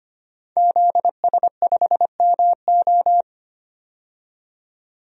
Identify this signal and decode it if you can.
Morse code, heard as a keyed tone, about 25 wpm, 700 Hz. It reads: ZS5MO
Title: DXCC-100 Call Signs - 5 Characters - 25wpm